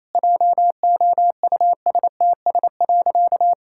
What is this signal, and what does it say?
Morse code, heard as a keyed tone, about 28 wpm, 700 Hz. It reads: JOUSTS.
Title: Pangram Challenge v2 - 28wpm